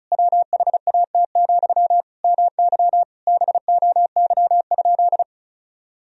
Morse code, heard as a keyed tone, about 34 wpm, 700 Hz. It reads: WHAT, MY BOY?
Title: Sentences from Top 1000 Words - Review Part 4 - 35wpm